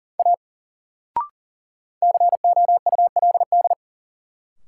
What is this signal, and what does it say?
Morse code, heard as a keyed tone, about 40 wpm, 700 Hz. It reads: A COULD
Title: Sentences from Top 100 Words - New Word Drill - 40wpm